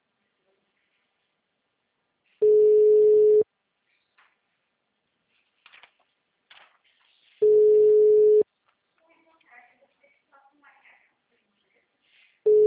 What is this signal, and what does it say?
Ringback tone (Europe, 425 Hz), with speech.